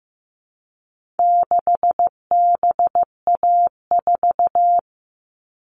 Morse code, heard as a keyed tone, about 15 wpm, 700 Hz. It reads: 6BA4
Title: All Cut Numbers - 4-Digit 15wpm